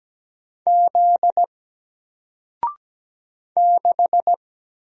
Morse code, heard as a keyed tone, about 17 wpm, 700 Hz. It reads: Z 6
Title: Single Letter-Number Alt 17wpm